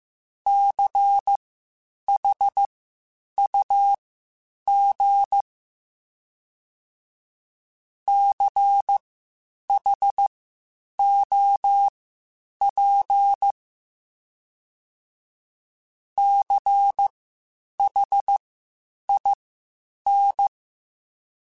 Morse code, heard as a keyed tone, about 15 wpm, spaced as for 8 wpm, 800 Hz. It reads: CHUG CHOP CHIN
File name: Spaced Alphas / Spaced Alphas mp3s / Cs sp @13